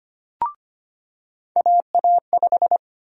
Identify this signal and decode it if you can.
Morse code, heard as a keyed tone, about 25 wpm, 700 Hz. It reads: AA5
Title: USA Call Sign Prefixes - 25wpm